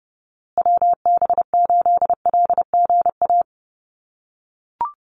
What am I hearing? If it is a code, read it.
Morse code, heard as a keyed tone, about 30 wpm, 700 Hz. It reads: WB8LGA
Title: USA Call Signs Repeated 2x 30wpm